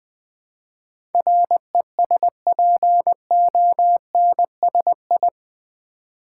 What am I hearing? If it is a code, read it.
Morse code, heard as a keyed tone, about 20 wpm, 700 Hz. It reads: RESPONSI
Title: Build A Word - 4 Letters At A Time - 11-Letter Words - 20wpm